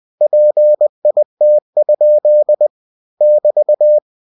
Morse code, heard as a keyed tone, about 20 wpm, 600 Hz. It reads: PIT? <BT>
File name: cw-books / winnie-the-pooh / winnie-20wpm-page-0035